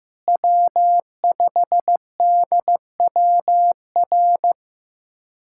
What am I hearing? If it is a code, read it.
Morse code, heard as a keyed tone, about 15 wpm, 700 Hz. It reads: W5DWR
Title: USA Callsign DE Callsign - 5 Characters - 15wpm